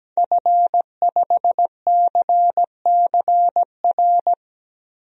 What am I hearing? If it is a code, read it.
Morse code, heard as a keyed tone, about 17 wpm, 700 Hz. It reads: F5CCR
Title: DXCC-100 Call Signs - 5 Characters - Repeated 3x - 17wpm